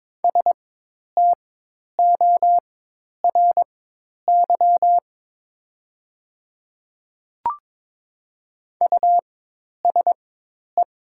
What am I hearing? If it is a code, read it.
Morse code, heard as a keyed tone, about 22 wpm, spaced as for 10 wpm, 700 Hz. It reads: STORY USE
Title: 4x Character Spacing - Top 300 Words - 22wpm